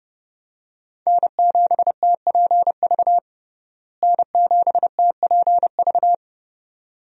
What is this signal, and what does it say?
Morse code, heard as a keyed tone, about 30 wpm, 700 Hz. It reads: N7TPV N7TPV
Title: USA Call Signs - 5 Characters - Repeated 2x - 30wpm